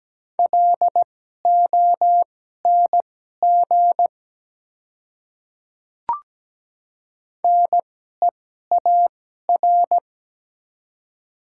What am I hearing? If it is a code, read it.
Morse code, heard as a keyed tone, about 17 wpm, spaced as for 12 wpm, 700 Hz. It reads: LONG NEAR
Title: 2x Character Spacing - Top 200 Words - 17wpm